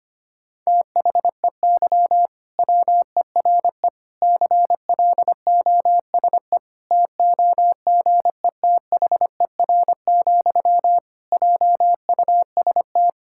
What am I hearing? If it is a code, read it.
Morse code, heard as a keyed tone, about 25 wpm, 700 Hz. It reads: THEY WERE CLOSE TOGETHER, JUST
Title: Sentences from Top 1000 Words - Review Part 6 - 25wpm